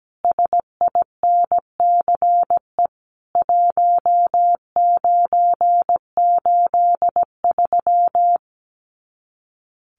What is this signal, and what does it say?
Morse code, heard as a keyed tone, about 17 wpm, 700 Hz. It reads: SINCE 1983
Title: Common QSO Phrases 17wpm